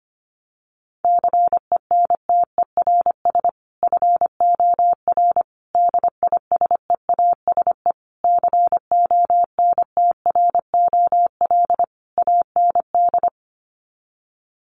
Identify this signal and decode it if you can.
Morse code, heard as a keyed tone, about 25 wpm, 700 Hz. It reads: CENTERS FOR DISEASE CONTROL AND
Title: Sets of 5 Words Encore2 25wpm